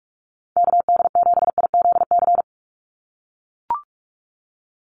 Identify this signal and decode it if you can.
Morse code, heard as a keyed tone, about 44 wpm, 700 Hz. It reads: KD7IZC
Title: USA Call Signs 45wpm